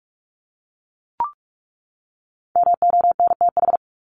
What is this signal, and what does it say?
Morse code, heard as a keyed tone, about 44 wpm, 700 Hz. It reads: MONTH
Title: LY Suffix - Top 100 Words with Base Words 45wpm